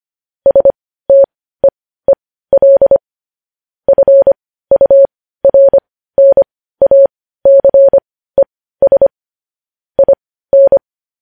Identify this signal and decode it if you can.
Morse code, heard as a keyed tone, about 25 wpm, spaced as for 15 wpm, 550 Hz. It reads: STEEL FURNACES IN
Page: news headlines in morse code at 15 wpm